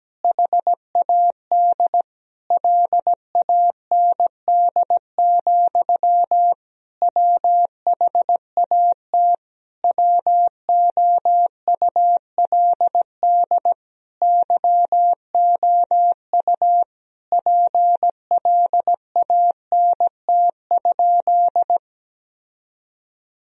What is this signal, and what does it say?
Morse code, heard as a keyed tone, about 17 wpm, 700 Hz. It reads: HAD LAND, WHAT WOULD YOU PLANT?
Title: Sentences from Top 1000 Words - Review Part 1 - 17wpm